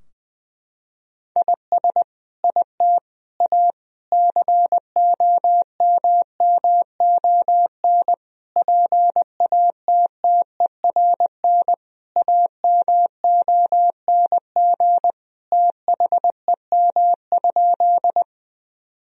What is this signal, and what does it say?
Morse code, heard as a keyed tone, about 21 wpm, 700 Hz. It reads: IS IT A COMMON PATTERN AMONG THEM?
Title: Sentences from Top 1000 Words - Review Part 2 - 20wpm